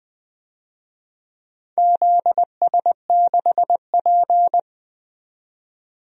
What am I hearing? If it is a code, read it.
Morse code, heard as a keyed tone, about 20 wpm, 700 Hz. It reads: ZS6P
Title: DXCC-100 Call Signs - 4 Characters - 20wpm